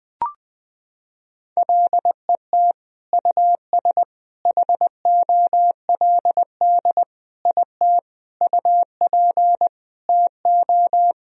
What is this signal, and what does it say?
Morse code, heard as a keyed tone, about 20 wpm, 700 Hz. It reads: LET US HOLD IT UP TO
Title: Sentences from Top 1000 Words - Review Part 2 - 20wpm